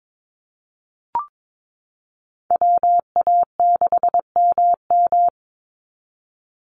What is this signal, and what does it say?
Morse code, heard as a keyed tone, about 22 wpm, 700 Hz. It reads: WA6MM
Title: NA SOTA Activator Callsigns 22wpm